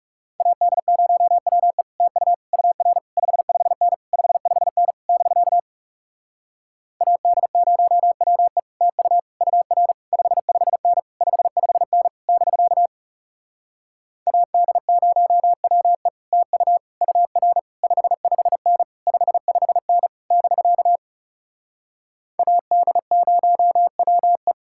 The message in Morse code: AD0WE TU UR 55N 55N <BK> AD0WE TU UR 55N 55N <BK> AD0WE TU UR 55N 55N <BK> AD0WE

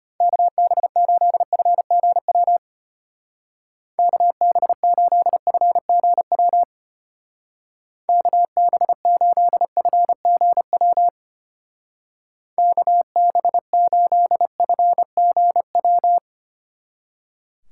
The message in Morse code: KB8FGW KB8FGW KB8FGW KB8FGW

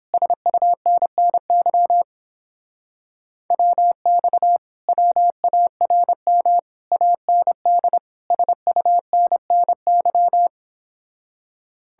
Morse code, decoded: SUNNY WX WARM AND SUNNY

